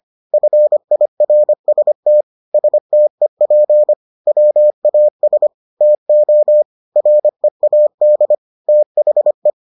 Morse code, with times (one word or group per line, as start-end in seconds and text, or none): 0.3-2.2 FIRST
2.5-3.9 STEP
4.3-5.5 WAS
5.8-6.6 TO
7.0-8.3 READ
8.7-9.5 THE